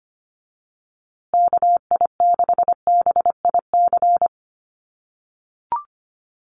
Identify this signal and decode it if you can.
Morse code, heard as a keyed tone, about 25 wpm, 700 Hz. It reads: KI6BIC
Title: USA Call Signs Repeated 3x 25wpm